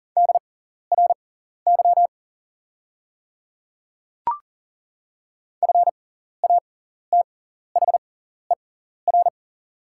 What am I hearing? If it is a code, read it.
Morse code, heard as a keyed tone, about 41 wpm, spaced as for 14 wpm, 700 Hz. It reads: DRY FATHER
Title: 6x Character Spacing - Top 500 Words - 40wpm